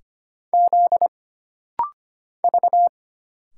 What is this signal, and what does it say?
Morse code, heard as a keyed tone, about 25 wpm, 700 Hz. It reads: Z V
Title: Single Letters - Rapid-Fire 25wpm